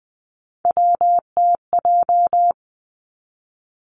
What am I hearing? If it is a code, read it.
Morse code, heard as a keyed tone, about 20 wpm, 700 Hz. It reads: WTJ
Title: Three Challenging Letter-Number Combinations 20wpm